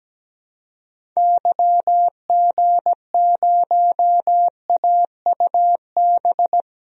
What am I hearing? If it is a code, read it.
Morse code, heard as a keyed tone, about 17 wpm, 700 Hz. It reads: YG0AUB
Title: DXCC-100 Call Signs - 6 Characters - 17wpm